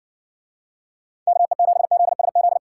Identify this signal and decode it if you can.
Morse code, heard as a keyed tone, about 57 wpm, 700 Hz. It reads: KE7ZSZ